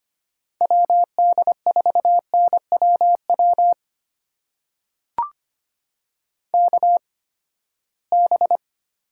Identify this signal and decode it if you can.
Morse code, heard as a keyed tone, about 25 wpm, spaced as for 17 wpm, 700 Hz. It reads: WD4NWW KB
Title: USA Call Signs - 6 Characters - 8x Character Spacing - 25wpm